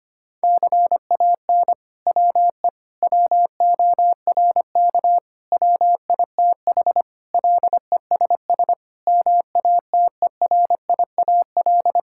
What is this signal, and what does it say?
Morse code, heard as a keyed tone, about 25 wpm, 700 Hz. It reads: CAN WE WORK WITH LESS MATERIAL
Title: Sentences from Top 1000 Words - Review Part 5 - 25wpm